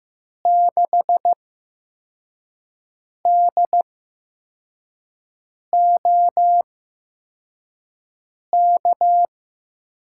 Morse code, heard as a keyed tone, about 15 wpm, spaced as for 4 wpm, 700 Hz. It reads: 6DOK
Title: USA Call Signs - 6 Characters - 8x Character Spacing - 15wpm